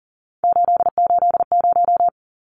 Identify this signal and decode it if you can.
Morse code, heard as a keyed tone, about 40 wpm, 700 Hz. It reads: 880
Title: Numbers - 3-Digit 40wpm